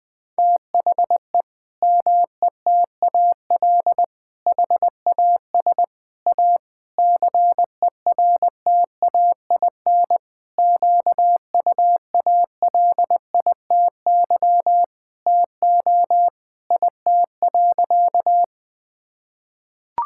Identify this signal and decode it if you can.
Morse code, heard as a keyed tone, about 20 wpm, 700 Hz. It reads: THE METAL HAS A CERTAIN QUALITY TO IT.
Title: Sentences from Top 1000 Words - Review Part 6 - 20wpm